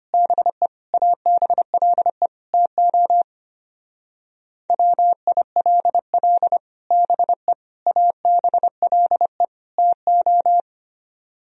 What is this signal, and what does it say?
Morse code, heard as a keyed tone, about 32 wpm, 700 Hz. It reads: BE ABLE TO WILL BE ABLE TO